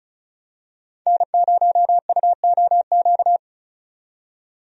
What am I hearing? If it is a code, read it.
Morse code, heard as a keyed tone, about 34 wpm, 700 Hz. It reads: N0UOQ
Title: USA Call Signs - 5 Characters - 35wpm